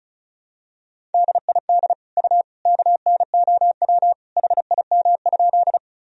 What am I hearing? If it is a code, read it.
Morse code, heard as a keyed tone, about 36 wpm, 700 Hz. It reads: DID U KNOW HIM?
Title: Sentences from Top 100 Words and Abbreviations 35wpm